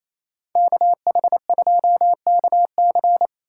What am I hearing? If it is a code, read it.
Morse code, heard as a keyed tone, about 28 wpm, 700 Hz. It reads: KH2KC
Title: USA Call Signs - 2x Character Spacing - 28wpm